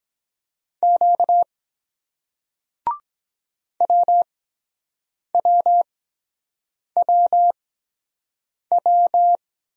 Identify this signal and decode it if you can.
Morse code, heard as a keyed tone, about 24 wpm, spaced as for 7 wpm, 700 Hz. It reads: Q WWWW